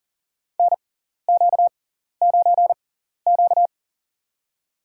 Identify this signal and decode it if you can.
Morse code, heard as a keyed tone, about 40 wpm, spaced as for 14 wpm, 700 Hz. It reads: NQ9Q
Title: USA Call Signs - 4 Characters - 6x Character Spacing - 40wpm